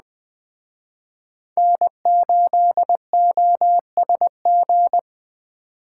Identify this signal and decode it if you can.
Morse code, heard as a keyed tone, about 23 wpm, 700 Hz. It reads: N8OSG